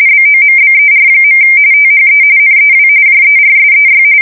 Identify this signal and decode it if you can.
RTTY, about 45.45 baud. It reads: BOMBER. THE NEW JERSEY
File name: fsk170 / USS-New-Jersey